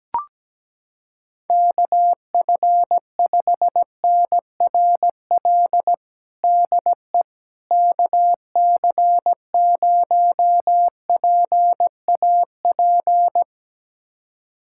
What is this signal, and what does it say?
Morse code, heard as a keyed tone, about 17 wpm, 700 Hz. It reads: KF5NRL DE KC0PAP
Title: USA Callsign DE Callsign 17wpm